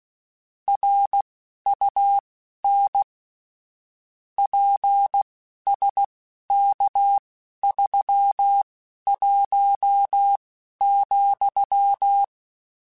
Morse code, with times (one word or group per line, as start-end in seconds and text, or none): 0.7-3.0 RUN
4.4-12.2 PSK31,